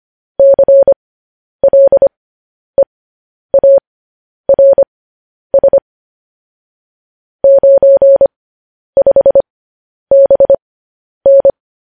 Morse code, heard as a keyed tone, about 25 wpm, spaced as for 10 wpm, 550 Hz. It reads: CLEARS 95BN
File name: podcast / News headlines-1x-25x10